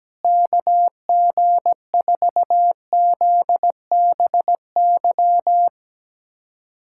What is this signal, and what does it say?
Morse code, heard as a keyed tone, about 17 wpm, 700 Hz. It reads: KG4ZBY
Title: USA Callsign DE Callsign 17wpm